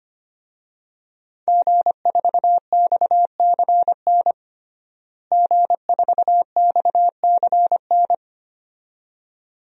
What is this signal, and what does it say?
Morse code, heard as a keyed tone, about 25 wpm, 700 Hz. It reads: G4XCN G4XCN
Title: DXCC-100 Call Signs - 5 Characters - Repeated 2x - 25wpm